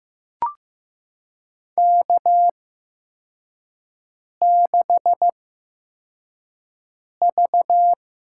Morse code, heard as a keyed tone, about 15 wpm, spaced as for 4 wpm, 700 Hz. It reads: K6V
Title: USA Call Signs - 4 Characters - 8x Character Spacing - 15wpm